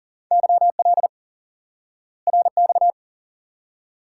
Morse code, heard as a keyed tone, about 40 wpm, 700 Hz. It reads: YL RX
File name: IQE Course 3 - Lesson 89 - YAGI - Two Elements - 40wpm